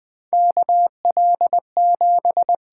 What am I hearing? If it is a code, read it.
Morse code, heard as a keyed tone, about 20 wpm, 700 Hz. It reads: KL7
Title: USA Call Signs - 6 Characters - 8x Character Spacing - 20wpm